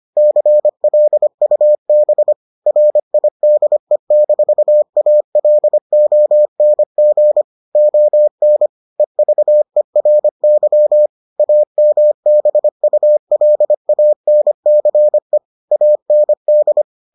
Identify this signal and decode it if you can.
Morse code, heard as a keyed tone, about 25 wpm, 600 Hz. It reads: CLUB RIDE-ALONG ON EVERY AMBULANCE AND